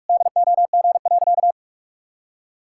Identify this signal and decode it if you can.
Morse code, heard as a keyed tone, about 46 wpm, 700 Hz. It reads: DOG.
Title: Pangram Challenge v2 - 45wpm